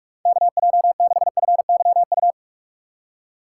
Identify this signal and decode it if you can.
Morse code, heard as a keyed tone, about 45 wpm, 700 Hz. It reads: KJ6FYU